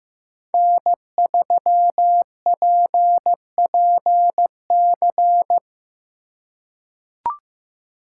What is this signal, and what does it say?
Morse code, heard as a keyed tone, about 15 wpm, 700 Hz. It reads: N3PPC
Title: USA Call Signs - 4x Character Spacing - 15wpm